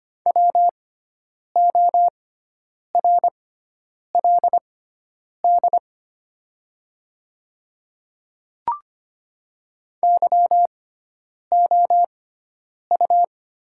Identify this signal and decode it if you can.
Morse code, heard as a keyed tone, about 25 wpm, spaced as for 9 wpm, 700 Hz. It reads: WORLD YOU
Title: 6x Character Spacing - Top 200 Words - 25wpm